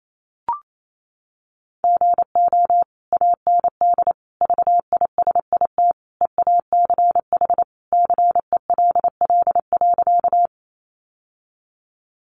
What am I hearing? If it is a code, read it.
Morse code, heard as a keyed tone, about 28 wpm, 700 Hz. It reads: GO AND VISIT EACH CELL.
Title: Sentences from Top 1100 Words 28wpm